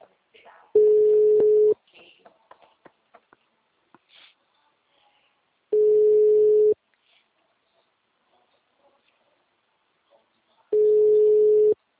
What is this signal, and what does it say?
Ringback tone (Europe, 425 Hz), with speech.
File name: out- -2019- - - .3252